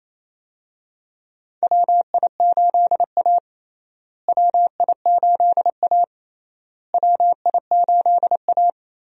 Morse code, heard as a keyed tone, about 28 wpm, 700 Hz. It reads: WI8A WI8A WI8A
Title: USA Call Signs - 4 Characters - Repeated 3x - 28wpm